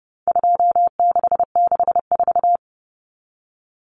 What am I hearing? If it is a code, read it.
Morse code, heard as a keyed tone, about 28 wpm, 700 Hz. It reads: 2664
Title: Common Cut Numbers - 4-Digit 30wpm